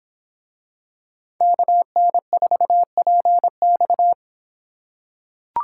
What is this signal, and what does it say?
Morse code, heard as a keyed tone, about 25 wpm, 700 Hz. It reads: KN4PX